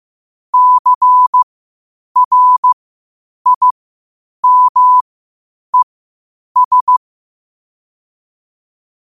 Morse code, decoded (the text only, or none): CRIMES